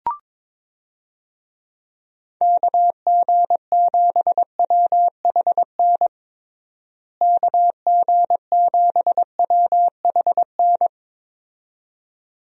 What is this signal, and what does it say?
Morse code, heard as a keyed tone, about 22 wpm, 700 Hz. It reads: KG7WHN KG7WHN
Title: USA Call Signs - 6 Characters - Repeated 2x - 22wpm